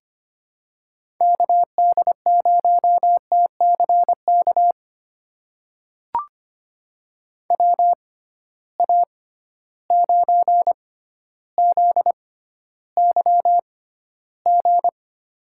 Morse code, heard as a keyed tone, about 25 wpm, spaced as for 13 wpm, 700 Hz. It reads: KD0TCK WA9ZYG
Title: USA Call Signs - 6 Characters - 6x Character Spacing - 25wpm